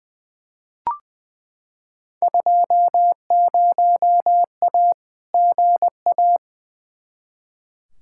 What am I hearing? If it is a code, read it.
Morse code, heard as a keyed tone, about 20 wpm, 700 Hz. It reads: 20A GA
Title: Field Day Exchanges Encore - 20wpm